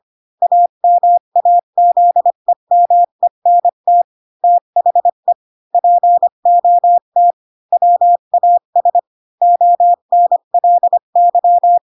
Morse code, heard as a keyed tone, about 25 wpm, 700 Hz. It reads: AMAZEMENT THE POT WAS ONLY
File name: Pinocchio / Pinocchio ch5 20 25 ik1hge